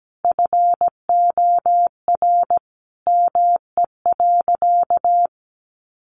Morse code, heard as a keyed tone, about 17 wpm, 700 Hz. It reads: FOR ME.
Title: Sentences from Top 1200 Words 17wpm